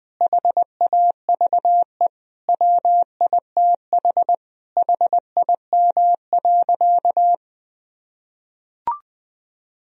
Morse code, heard as a keyed tone, about 20 wpm, 700 Hz. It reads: HAVE WITH HIM.
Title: Sentences from Top 1000 Words - Review Part 1 - 20wpm